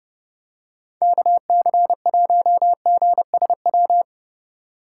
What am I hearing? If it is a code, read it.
Morse code, heard as a keyed tone, about 28 wpm, 700 Hz. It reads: KC1GSW
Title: Speed Racing - USA Call Signs 30wpm